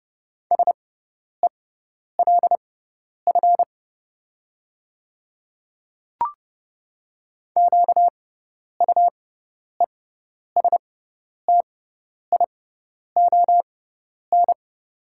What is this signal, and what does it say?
Morse code, heard as a keyed tone, about 32 wpm, spaced as for 10 wpm, 700 Hz. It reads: SELF QUESTION
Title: 6x Character Spacing - Top 500 Words - 30wpm